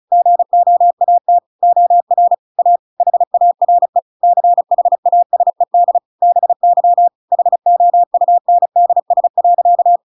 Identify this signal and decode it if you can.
Morse code, heard as a keyed tone, about 34 wpm, 700 Hz. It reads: GOAT OR A HARE CHASED BY HOUNDS.